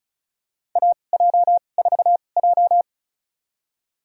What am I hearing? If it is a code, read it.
Morse code, heard as a keyed tone, about 34 wpm, spaced as for 25 wpm, 690 Hz. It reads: AJ4J
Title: USA Call Signs - 4 Characters - 2x Character Spacing - 35wpm